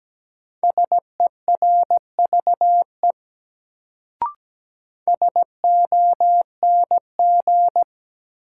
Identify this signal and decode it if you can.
Morse code, heard as a keyed tone, about 17 wpm, 700 Hz. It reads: SERVE SONG
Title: Sentences from Top 400 Words - New Word Drill - 17wpm